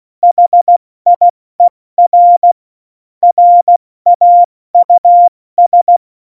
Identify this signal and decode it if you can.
Morse code, heard as a keyed tone, about 16 wpm, 700 Hz. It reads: HIER RAUS
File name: Morsekurs / Witze / Witz 18. September 2018, 70 F80